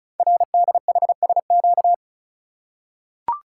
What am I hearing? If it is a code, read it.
Morse code, heard as a keyed tone, about 36 wpm, 700 Hz. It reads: RDHSQ
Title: Code Groups - 5 Letters - 35wpm